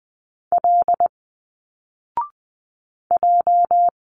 Morse code, heard as a keyed tone, about 20 wpm, 700 Hz. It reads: L J
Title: Single Letters 20wpm